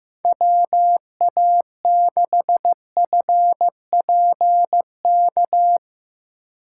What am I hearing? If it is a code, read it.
Morse code, heard as a keyed tone, about 15 wpm, 700 Hz. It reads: WA6FPK
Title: USA Call Signs - 6 Characters - 4x Character Spacing - 15wpm